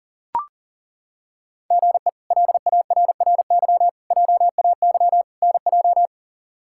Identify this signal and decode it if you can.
Morse code, heard as a keyed tone, about 37 wpm, 700 Hz. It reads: GE LARRY JAY NJ
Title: K1USN SST QSO Practice - 40wpm